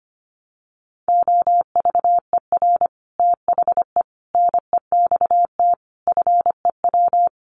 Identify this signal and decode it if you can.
Morse code, heard as a keyed tone, about 25 wpm, 700 Hz. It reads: OVER THE NEXT FEW
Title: Sets of 5 Words Encore 25wpm